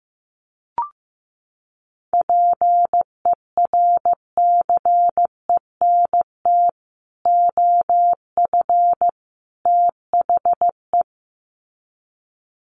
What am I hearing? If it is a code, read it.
Morse code, heard as a keyed tone, about 15 wpm, 700 Hz. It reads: PERCENT OF THE
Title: Sets of 3 Words 15wpm